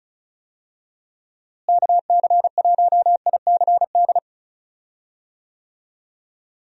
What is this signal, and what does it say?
Morse code, heard as a keyed tone, about 36 wpm, 700 Hz. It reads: KC1ICD
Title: USA Call Signs - 6 Characters - Repeated 2x - 35wpm